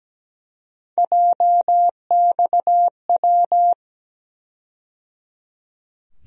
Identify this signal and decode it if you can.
Morse code, heard as a keyed tone, about 17 wpm, 700 Hz. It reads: JXW
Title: USA Call Sign Postfixes - 3 Letters 17wpm